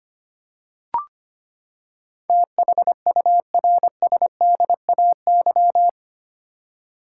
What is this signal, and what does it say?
Morse code, heard as a keyed tone, about 25 wpm, 700 Hz. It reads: THURSDAY
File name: Eight Letter Words - 25wpm